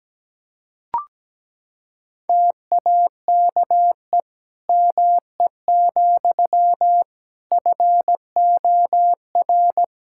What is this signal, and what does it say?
Morse code, heard as a keyed tone, about 17 wpm, 700 Hz. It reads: TAKE ME, FOR
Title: Sentences from Top 1000 Words - Review Part 5 - 17wpm